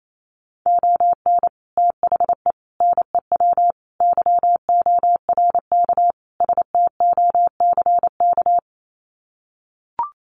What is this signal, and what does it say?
Morse code, heard as a keyed tone, about 28 wpm, 690 Hz. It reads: ON THE NEW YORK STOCK
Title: Sets of 5 Words Encore 28wpm